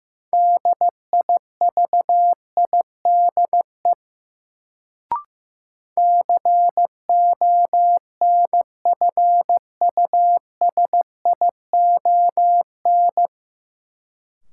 Morse code, heard as a keyed tone, about 15 wpm, 700 Hz. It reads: DIVIDE CONFUSION
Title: Sentences from Top 1600 Words - New Word Drill - 15wpm